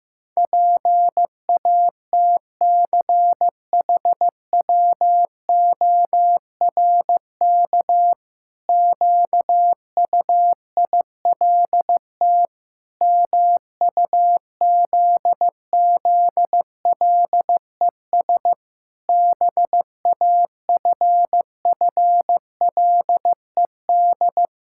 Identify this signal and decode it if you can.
Morse code, heard as a keyed tone, about 16 wpm, 700 Hz. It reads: PATCHWORK QUILT MUZZLES BAFFLED
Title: Pangram Challenge v2 - 15wpm